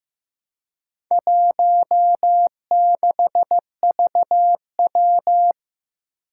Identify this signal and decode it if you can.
Morse code, heard as a keyed tone, about 15 wpm, 700 Hz. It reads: 16VW
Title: Four Challenging Letter-Number Combinations 15wpm